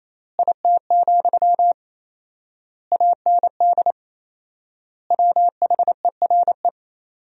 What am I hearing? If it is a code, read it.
Morse code, heard as a keyed tone, about 28 wpm, 700 Hz. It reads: IT, AND WHERE
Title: 4x Word-Spacing - Sentences from Top 300 Words - 28wpm